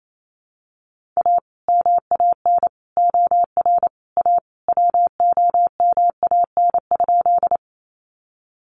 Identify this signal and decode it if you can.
Morse code, heard as a keyed tone, about 28 wpm, 700 Hz. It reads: A MAN OR A WOMAN?
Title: Sentences from Top 1100 Words 28wpm